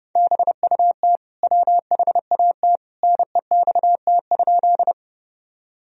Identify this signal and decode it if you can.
Morse code, heard as a keyed tone, about 32 wpm, 700 Hz. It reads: BUT WHAT NEXT?
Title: Sentences from Top 1000 Words - Review Part 1 - 30wpm